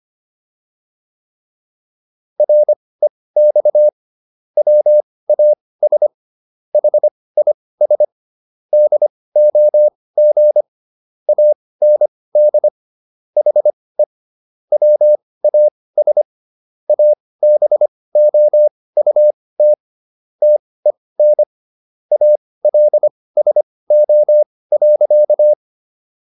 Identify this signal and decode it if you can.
Morse code, heard as a keyed tone, about 25 wpm, spaced as for 18 wpm, 600 Hz. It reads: REX WAS HIS DOG AND HE WAS ABOUT TEN ALSO.